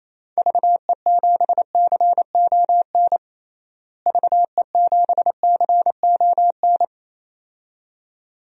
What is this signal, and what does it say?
Morse code, heard as a keyed tone, about 27 wpm, 700 Hz. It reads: VE7CON VE7CON
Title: DXCC-100 Call Signs - 6 Characters - Repeated 2x - 28wpm